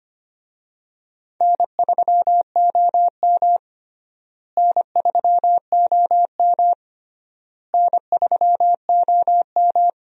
Morse code, heard as a keyed tone, about 25 wpm, 700 Hz. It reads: N3OM N3OM N3OM
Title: USA Call Signs - 4 Characters - Repeated 3x - 25wpm